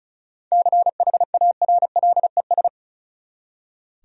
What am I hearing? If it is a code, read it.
Morse code, heard as a keyed tone, about 36 wpm, 700 Hz. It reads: CHARLES
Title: USA Common Names - 35wpm